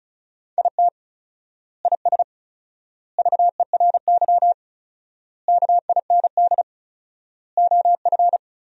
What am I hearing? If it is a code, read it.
Morse code, heard as a keyed tone, about 35 wpm, 700 Hz. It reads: IT IS VERY KIND OF
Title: 4x Word-Spacing - Sentences from Top 200 Words - 35wpm